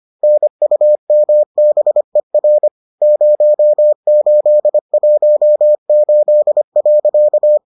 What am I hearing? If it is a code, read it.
Morse code, heard as a keyed tone, about 25 wpm, 600 Hz. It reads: NUMBER 0818.